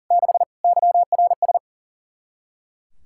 Morse code, heard as a keyed tone, about 40 wpm, 700 Hz. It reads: 6 YRS
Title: K1USN SST QSO Practice 40wpm